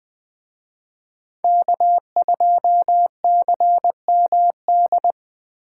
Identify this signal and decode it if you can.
Morse code, heard as a keyed tone, about 24 wpm, 700 Hz. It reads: K2CMD
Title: Speed Racing - USA 1x3 Call Signs 20wpm